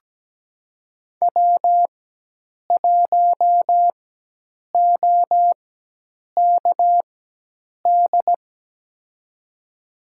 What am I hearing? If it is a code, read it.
Morse code, heard as a keyed tone, about 17 wpm, spaced as for 8 wpm, 700 Hz. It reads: W1OKD
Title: USA Call Signs - 4x Character Spacing - 17wpm